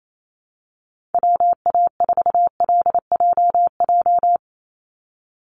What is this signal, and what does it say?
Morse code, heard as a keyed tone, about 27 wpm, 700 Hz. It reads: WA4LJJ
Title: Call Signs - 28wpm